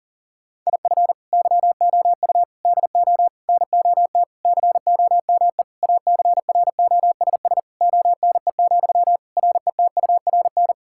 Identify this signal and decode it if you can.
Morse code, heard as a keyed tone, about 37 wpm, 690 Hz. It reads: IF YOU DO NOT COME ACROSS ONE, RETURN
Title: Sentences from Top 1000 Words - Review Part 5 - 40wpm